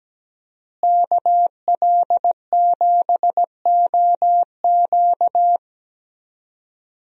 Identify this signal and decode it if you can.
Morse code, heard as a keyed tone, about 17 wpm, 700 Hz. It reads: KL7OQ
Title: DXCC-100 Call Signs - 5 Characters - Repeated 3x - 17wpm